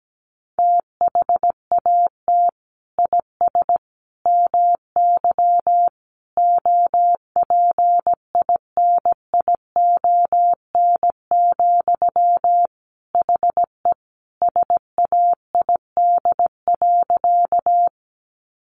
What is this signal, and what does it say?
Morse code, heard as a keyed tone, about 17 wpm, 700 Hz. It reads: THAT IS MY OPINION, HE SAID.
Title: Sentences from Top 1200 Words 17wpm